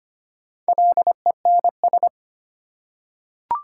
Morse code, heard as a keyed tone, about 25 wpm, 690 Hz. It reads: LENS
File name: practice / 25wpm / Easy Words with Hints - 25wpm